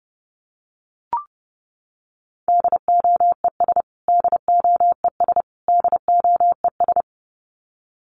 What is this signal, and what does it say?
Morse code, heard as a keyed tone, about 30 wpm, 700 Hz. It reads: DOES DOES DOES
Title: Top 500 Words Repeated 30wpm